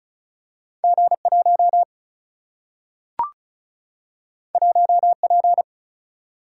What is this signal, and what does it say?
Morse code, heard as a keyed tone, about 36 wpm, 700 Hz. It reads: G 1 1 P
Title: Single-Letter Single-Number 35wpm